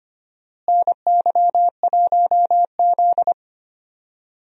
Morse code, heard as a keyed tone, about 25 wpm, 700 Hz. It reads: NY1Z
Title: USA Call Signs - 4 Characters - Repeated 3x - 25wpm